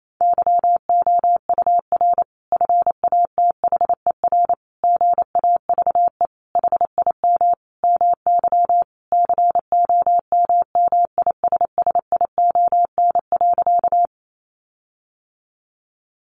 Morse code, heard as a keyed tone, about 28 wpm, 700 Hz. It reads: YOUR FATHER GAVE HIM MY COMMISSION.
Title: Sentences from Top 1400 Words 28wpm